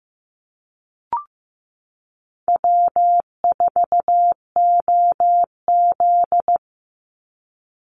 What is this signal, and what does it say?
Morse code, heard as a keyed tone, about 15 wpm, 700 Hz. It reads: W4OZ
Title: USA Call Signs 15wpm